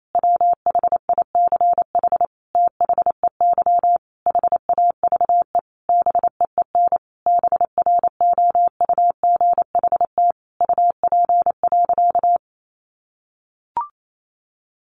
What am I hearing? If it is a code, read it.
Morse code, heard as a keyed tone, about 28 wpm, 700 Hz. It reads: WHICH THEY HAVE BEEN BROUGHT UP.
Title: Sentences from Top 1500 Words 28wpm